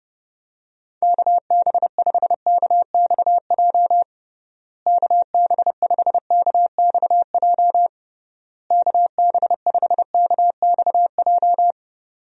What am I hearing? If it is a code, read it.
Morse code, heard as a keyed tone, about 30 wpm, 700 Hz. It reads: KB5KXJ KB5KXJ KB5KXJ
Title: USA Call Signs - 6 Characters - Repeated 3x - 30wpm